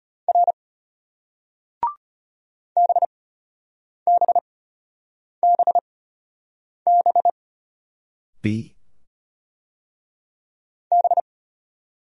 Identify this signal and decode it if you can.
Morse code, heard as a keyed tone, about 34 wpm, spaced as for 8 wpm, 700 Hz. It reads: R BBBB B